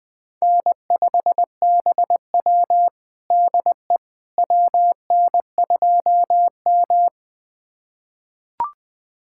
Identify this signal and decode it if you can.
Morse code, heard as a keyed tone, about 21 wpm, 700 Hz. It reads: N5BW DE WN2M
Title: USA Callsign DE Callsign - 4 Characters - 20wpm